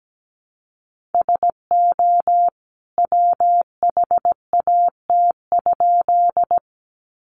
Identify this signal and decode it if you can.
Morse code, heard as a keyed tone, about 17 wpm, 700 Hz. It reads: SO WHAT?
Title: A Sound Of Thunder 17wpm